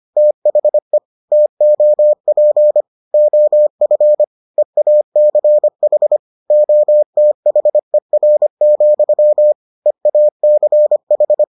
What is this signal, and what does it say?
Morse code, heard as a keyed tone, about 25 wpm, 600 Hz. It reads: THE TOP OF EACH OTHER, EACH